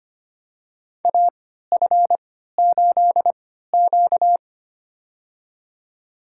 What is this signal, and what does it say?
Morse code, heard as a keyed tone, about 25 wpm, spaced as for 14 wpm, 700 Hz. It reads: AF8Q
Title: USA Call Signs - 4 Characters - 3x Character Spacing - 25wpm